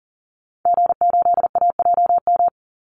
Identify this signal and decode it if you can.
Morse code, heard as a keyed tone, about 40 wpm, 700 Hz. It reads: G8AJM
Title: Call Signs Repeated 2x - 40wpm